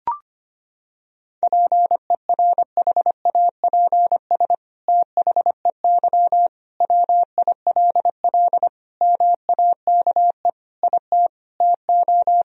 Morse code, heard as a keyed tone, about 25 wpm, 700 Hz. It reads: PERHAPS THEY WILL MAKE IT TO
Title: Sentences from Top 1000 Words - Review Part 2 - 25wpm